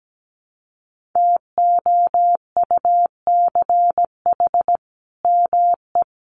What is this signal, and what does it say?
Morse code, heard as a keyed tone, about 17 wpm, 700 Hz. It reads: TOUCH ME
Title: Sentences from Top 1200 Words 17wpm